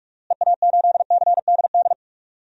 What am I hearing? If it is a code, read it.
Morse code, heard as a keyed tone, about 46 wpm, 700 Hz. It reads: EA8CDD
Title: DXCC-100 Call Signs - 6 Characters - Repeated 3x - 45wpm